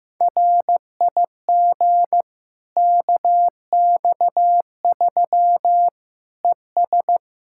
Morse code, heard as a keyed tone, about 15 wpm, 700 Hz. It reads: RIG KX3 ES
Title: K1USN SST QSO Practice 15wpm